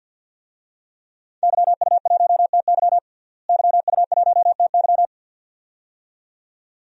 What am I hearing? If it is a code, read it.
Morse code, heard as a keyed tone, about 51 wpm, 700 Hz. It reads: YU1TY YU1TY
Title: DXCC-100 Call Signs - 5 Characters - Repeated 2x - 50wpm